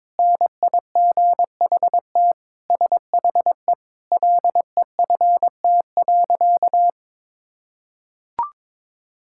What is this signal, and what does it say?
Morse code, heard as a keyed tone, about 22 wpm, 700 Hz. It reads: NIGHT SHE LEFT.
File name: practice / 22wpm / Sentences with 4-Word Phrases - 22wpm